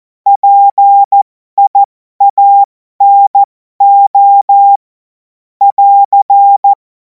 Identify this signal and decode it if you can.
Morse code, heard as a keyed tone, about 14 wpm, spaced as for 12 wpm, 800 Hz. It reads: PIANO <AR>
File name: Morsekurs / Witze